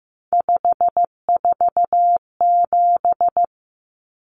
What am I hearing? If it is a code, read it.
Morse code, heard as a keyed tone, about 15 wpm, 700 Hz. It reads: 547
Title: Common Cut Numbers - 3-Digit 15wpm